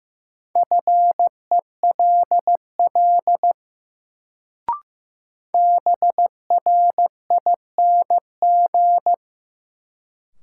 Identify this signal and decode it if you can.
Morse code, heard as a keyed tone, about 15 wpm, 700 Hz. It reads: FELL BRING
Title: Sets of 5 Words Encore - New Word Drill - 15wpm